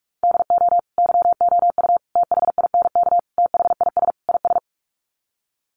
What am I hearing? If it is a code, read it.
Morse code, heard as a keyed tone, about 44 wpm, 690 Hz. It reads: DO YOU THINK THIS IS
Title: Sets of 5 Words Encore2 45wpm